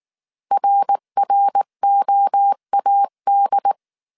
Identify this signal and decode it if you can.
Morse code, heard as a keyed tone, about 19 wpm, 800 Hz. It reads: RROAD